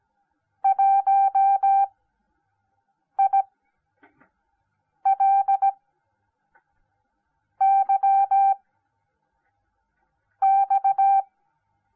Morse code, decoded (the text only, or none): 1ILYX